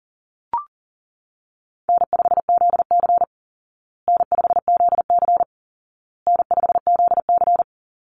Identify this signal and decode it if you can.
Morse code, heard as a keyed tone, about 40 wpm, 700 Hz. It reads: N5ZC N5ZC N5ZC
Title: Easy Call Signs Repeated 3x - 40wpm